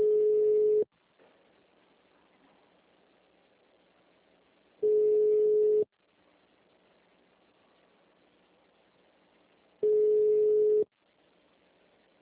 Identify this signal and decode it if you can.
Ringback tone (Europe, 425 Hz).
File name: out- -2004- - - .6